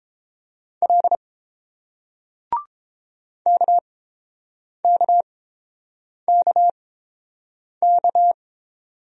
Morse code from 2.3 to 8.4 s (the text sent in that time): KKKK